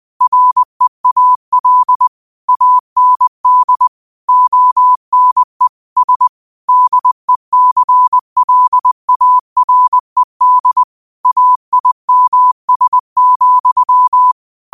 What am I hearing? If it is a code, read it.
Morse code, heard as a keyed tone, about 20 wpm, 1000 Hz. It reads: REAL AND ONE S DECLARED AIMS,